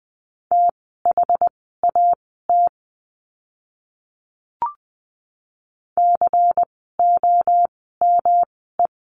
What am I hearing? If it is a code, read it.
Morse code, heard as a keyed tone, about 20 wpm, spaced as for 15 wpm, 700 Hz. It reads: THAT COME
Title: 206 Lesson 33f - Letter j - Words 20wpm